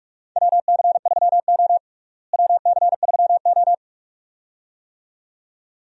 Morse code, heard as a keyed tone, about 43 wpm, 700 Hz. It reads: WC3O WC3O
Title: USA Call Signs - 4 Characters - Repeated 2x - 45wpm